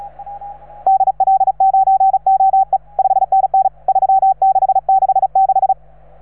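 Morse code, heard as a keyed tone, about 35 wpm, 770 Hz. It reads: DL9OE 5NN 3666
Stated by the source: Morse from station 3V9A.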